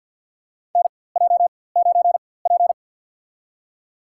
Morse code, heard as a keyed tone, about 52 wpm, spaced as for 23 wpm, 700 Hz. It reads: NJ9P
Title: USA Call Signs - 4x Character Spacing - 50wpm